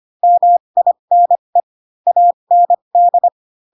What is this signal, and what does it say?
Morse code, heard as a keyed tone, about 25 wpm, spaced as for 22 wpm, 700 Hz. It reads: MINE AND
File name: Pinocchio / Pinocchio ch2 18 25 ik1hge